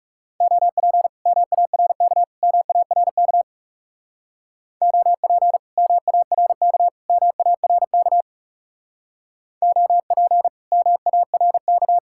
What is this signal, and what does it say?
Morse code, heard as a keyed tone, about 36 wpm, 700 Hz. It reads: OP MARK MARK OP MARK MARK OP MARK
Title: Speed-Racing Common QSO Phrases Encore - 30wpm